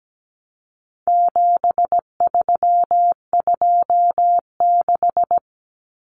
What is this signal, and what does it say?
Morse code, heard as a keyed tone, about 17 wpm, 700 Hz. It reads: 7326
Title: Common Cut Numbers - 4-Digit 17wpm